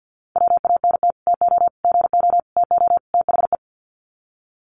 Morse code, heard as a keyed tone, about 50 wpm, 700 Hz. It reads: WANT TO GO TO THE
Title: Sets of 5 Words Encore2 50wpm